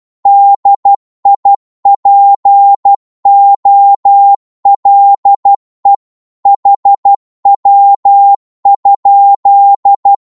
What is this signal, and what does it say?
Morse code, heard as a keyed tone, about 12 wpm, 800 Hz. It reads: DIPOLE HW?
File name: Morse / qso-12wpm